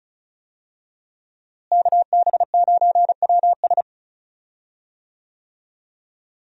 Morse code, heard as a keyed tone, about 36 wpm, 700 Hz. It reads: KB9WS
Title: USA Call Signs - 5 Characters - Repeated 2x - 35wpm